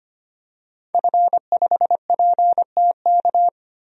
Morse code, heard as a keyed tone, about 25 wpm, 700 Hz. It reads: F5PTK
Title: DXCC-100 Call Signs - 5 Characters - Repeated 2x - 25wpm